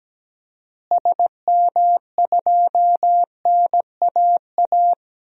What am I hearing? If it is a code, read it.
Morse code, heard as a keyed tone, about 17 wpm, 700 Hz. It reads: SM2NAA
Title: DXCC-100 Call Signs - 6 Characters - Repeated 3x - 17wpm